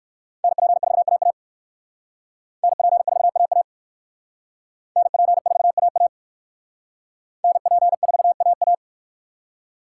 Morse code, heard as a keyed tone, about 64 wpm, 700 Hz. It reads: NP4AA NP4AA NP4AA NP4AA